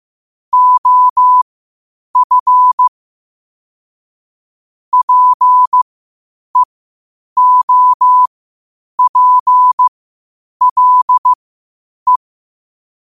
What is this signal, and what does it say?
Morse code, heard as a keyed tone, about 15 wpm, spaced as for 8 wpm, 1000 Hz. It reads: OF PEOPLE
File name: qotdfiles / 8wpm